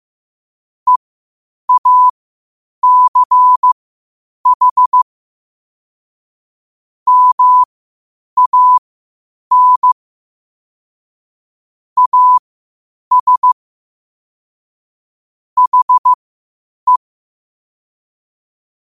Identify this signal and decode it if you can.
Morse code, heard as a keyed tone, about 15 wpm, spaced as for 8 wpm, 1000 Hz. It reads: EACH MAN AS HE
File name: qotdfiles / 8wpm